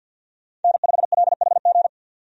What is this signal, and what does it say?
Morse code, heard as a keyed tone, about 52 wpm, 700 Hz. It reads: N5LHG
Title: USA Call Signs - 5 Characters - 50wpm